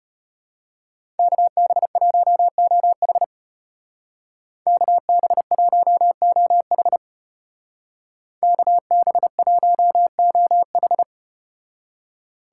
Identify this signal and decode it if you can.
Morse code, heard as a keyed tone, about 34 wpm, 700 Hz. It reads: KB1OH KB1OH KB1OH